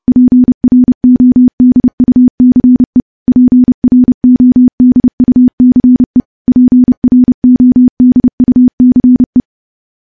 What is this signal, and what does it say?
Morse code, heard as a keyed tone, about 30 wpm, 250 Hz. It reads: PRODUCE PRODUCE PRODUCE
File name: morse / morse-training-30WPM